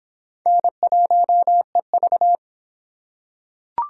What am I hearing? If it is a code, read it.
Morse code, heard as a keyed tone, about 23 wpm, 700 Hz. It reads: N1EV